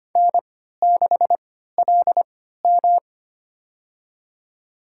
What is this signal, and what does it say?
Morse code, heard as a keyed tone, about 25 wpm, spaced as for 14 wpm, 700 Hz. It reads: N6LM
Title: USA Call Signs - 4 Characters - 3x Character Spacing - 25wpm